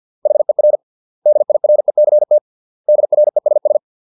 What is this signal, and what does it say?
Morse code, heard as a keyed tone, about 50 wpm, 600 Hz. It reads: HER DIRECT DRESS
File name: 632 words / 632 Words 50WPM